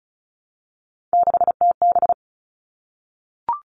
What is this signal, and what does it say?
Morse code, heard as a keyed tone, about 34 wpm, 700 Hz. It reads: 6TB
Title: All Cut Numbers - 3-Digit 35wpm